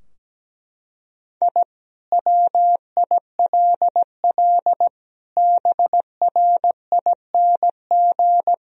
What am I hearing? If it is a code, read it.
Morse code, heard as a keyed tone, about 17 wpm, 700 Hz. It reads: I WILL BRING
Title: Sentences from Top 1000 Words - Review Part 3 - 17wpm